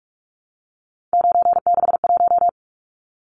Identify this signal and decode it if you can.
Morse code, heard as a keyed tone, about 44 wpm, 700 Hz. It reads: 961
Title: Numbers - 3-Digit 45wpm